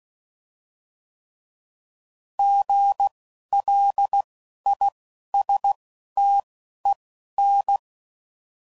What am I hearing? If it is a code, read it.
Morse code, heard as a keyed tone, about 16 wpm, spaced as for 11 wpm, 800 Hz. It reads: GLISTEN
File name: Prefixes / By First Letter